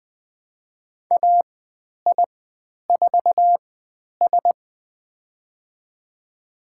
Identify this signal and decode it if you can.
Morse code, heard as a keyed tone, about 20 wpm, spaced as for 10 wpm, 700 Hz. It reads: AI4S
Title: 109 Lesson 18g - Letter m - Call Signs 20wpm